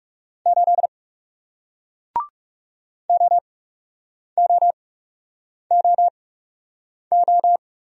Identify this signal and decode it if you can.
Morse code, heard as a keyed tone, about 41 wpm, spaced as for 8 wpm, 700 Hz. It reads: 8 OOOO